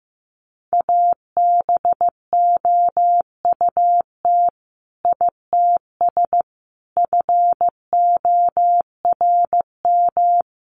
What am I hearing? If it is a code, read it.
Morse code, heard as a keyed tone, about 15 wpm, 700 Hz. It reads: ABOUT ITS FORM
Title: Sentences from Top 1200 Words 15wpm